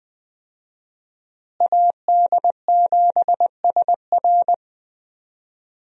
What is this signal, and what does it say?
Morse code, heard as a keyed tone, about 21 wpm, 700 Hz. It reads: AD7SR
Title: USA Call Signs - 5 Characters - Repeated 3x - 20wpm